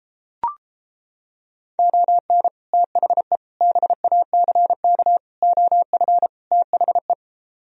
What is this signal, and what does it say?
Morse code, heard as a keyed tone, about 29 wpm, 700 Hz. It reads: ON THE BACK OF THE